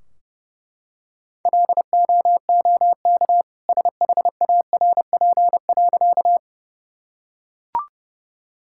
Morse code, heard as a keyed tone, about 32 wpm, 700 Hz. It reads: LOOK SHARP.
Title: Sentences from Top 1600 Words 30wpm